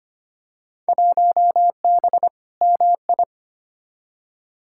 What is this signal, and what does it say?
Morse code, heard as a keyed tone, about 25 wpm, 700 Hz. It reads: 1B MI
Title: Field Day Exchanges - 25wpm